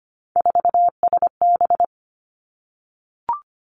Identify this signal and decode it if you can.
Morse code, heard as a keyed tone, about 25 wpm, 700 Hz. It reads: 4SB
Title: Three Challenging Letter-Number Combinations 25wpm